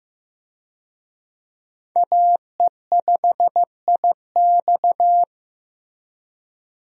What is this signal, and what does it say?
Morse code, heard as a keyed tone, about 15 wpm, 700 Hz. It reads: AE5IX
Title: USA Call Signs - 5 Characters - Repeated 3x - 15wpm